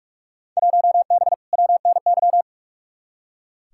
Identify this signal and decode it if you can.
Morse code, heard as a keyed tone, about 46 wpm, 700 Hz. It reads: 1B WNY
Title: Field Day Exchanges Encore - 45wpm